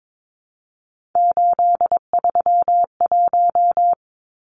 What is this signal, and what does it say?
Morse code, heard as a keyed tone, about 22 wpm, 700 Hz. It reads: 831
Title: Numbers - 3-Digit 22wpm